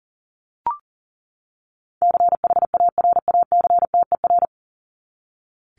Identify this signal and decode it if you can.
Morse code, heard as a keyed tone, about 40 wpm, 700 Hz. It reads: CHARACTER
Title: Top 1001-1250 Words 40wpm